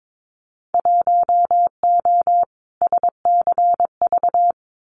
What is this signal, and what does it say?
Morse code, heard as a keyed tone, about 22 wpm, 700 Hz. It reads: 1O SCV
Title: Winter Field Day Exchanges 22wpm